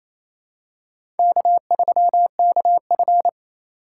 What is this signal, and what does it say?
Morse code, heard as a keyed tone, about 28 wpm, 700 Hz. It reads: K3KF
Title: USA Call Signs - 4 Characters - 2x Character Spacing - 28wpm